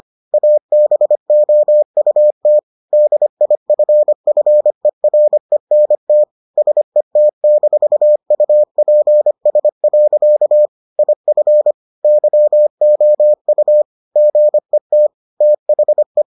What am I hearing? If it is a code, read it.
Morse code, heard as a keyed tone, about 25 wpm, 600 Hz. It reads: ABOUT DIFFERENT SET-UPS. IF YOU GET THE